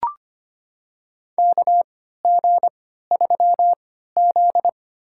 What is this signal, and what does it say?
Morse code, heard as a keyed tone, about 25 wpm, spaced as for 14 wpm, 700 Hz. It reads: KG3Z
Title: USA Call Signs - 4 Characters - 3x Character Spacing - 25wpm